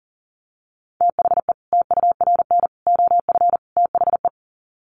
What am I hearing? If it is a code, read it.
Morse code, heard as a keyed tone, about 40 wpm, 700 Hz. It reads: THE TURN OF THE
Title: Sets of 4 Words 40wpm